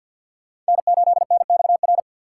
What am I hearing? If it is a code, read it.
Morse code, heard as a keyed tone, about 51 wpm, 700 Hz. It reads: N8NXR
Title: USA Call Signs - 5 Characters - 3x Character Spacing - 50wpm